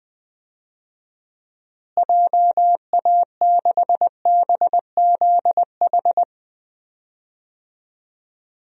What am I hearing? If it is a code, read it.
Morse code, heard as a keyed tone, about 20 wpm, 700 Hz. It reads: JA6BZH
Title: DXCC-100 Call Signs - 6 Characters - Repeated 2x - 20wpm